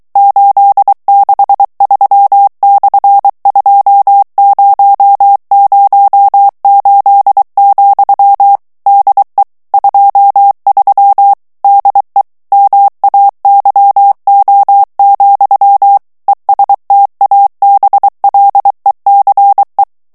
Morse code, heard as a keyed tone, about 23 wpm, 800 Hz. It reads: 863/2008, DE 23 DE MAYO, ESTABLECE
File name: morse mp3 / ESPAGNOL TEXTE EN CLAIR 2 VITESSE 22 MOTS MINUTES